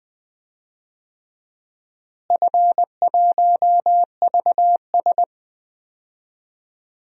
Morse code, heard as a keyed tone, about 21 wpm, 700 Hz. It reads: F1VS
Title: DXCC-100 Call Signs - 4 Characters - Repeated 2x - 20wpm